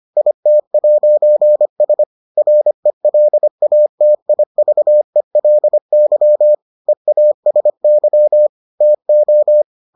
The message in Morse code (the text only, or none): IT'S RELATIVELY EASY TO